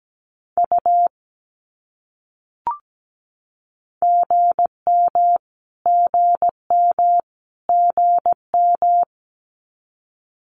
Morse code, heard as a keyed tone, about 17 wpm, 700 Hz. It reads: U GM GM GM
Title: Abbreviations Repeated 3x - 17wpm